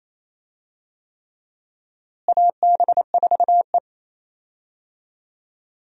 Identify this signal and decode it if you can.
Morse code, heard as a keyed tone, about 28 wpm, 700 Hz. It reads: AB4E
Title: USA Call Signs - 4 Characters - Repeated 3x - 28wpm